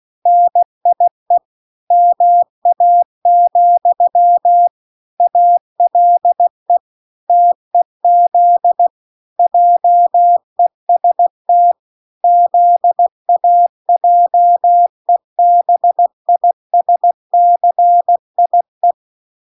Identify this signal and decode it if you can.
Morse code, heard as a keyed tone, about 16 wpm, 700 Hz. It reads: NIE MA, ALE TEZ JEST ZAJEBISCIE